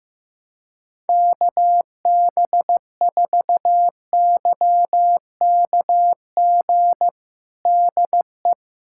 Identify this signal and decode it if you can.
Morse code, heard as a keyed tone, about 15 wpm, 700 Hz. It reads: KB4YKG DE
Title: USA Callsign DE Callsign 15wpm